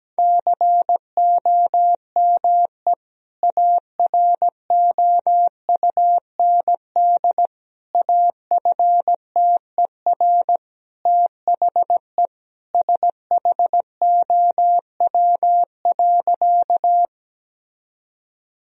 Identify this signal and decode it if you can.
Morse code, heard as a keyed tone, about 17 wpm, 700 Hz. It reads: COME AROUND AFTER THE SHOW.
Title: Sentences from Top 1000 Words - Review Part 3 - 17wpm